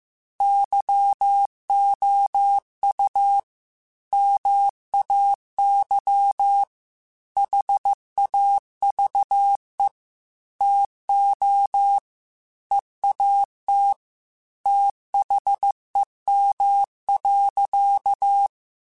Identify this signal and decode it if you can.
Morse code, heard as a keyed tone, about 15 wpm, 800 Hz. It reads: YOU MAY HAVE TO EAT THEM.
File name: Chuckles / Chuckles-mp3s / SoftWords @13